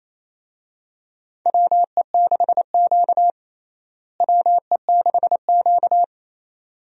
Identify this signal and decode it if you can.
Morse code, heard as a keyed tone, about 28 wpm, 700 Hz. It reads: WE6Q WE6Q
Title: USA Call Signs - 4 Characters - Repeated 2x - 28wpm